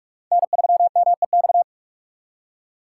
Morse code, heard as a keyed tone, about 46 wpm, 700 Hz. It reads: N3MEX
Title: USA Call Signs - 4x Character Spacing - 45wpm